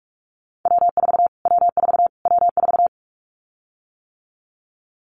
Morse code, heard as a keyed tone, about 44 wpm, 700 Hz. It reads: W4 W4 W4
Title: Callsign Prefixes for Top 25 DXCC - Repeated 3x - 45wpm